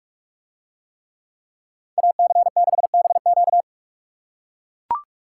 Morse code, heard as a keyed tone, about 44 wpm, 700 Hz. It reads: AC6BQ